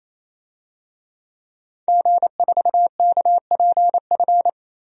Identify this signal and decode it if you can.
Morse code, heard as a keyed tone, about 28 wpm, 700 Hz. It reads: G4KPF
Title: DXCC-100 Call Signs - 5 Characters - Repeated 3x - 28wpm